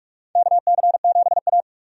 Morse code, heard as a keyed tone, about 46 wpm, 700 Hz. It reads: KC7A
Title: USA Call Signs - 4 Characters - 45wpm